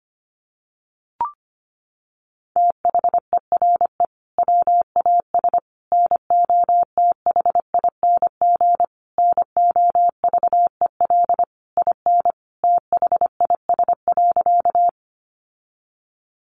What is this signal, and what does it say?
Morse code, heard as a keyed tone, about 25 wpm, 700 Hz. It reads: THERE WAS NOTHING NOVEL IN THIS.
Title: Sentences from Top 1300 Words 25wpm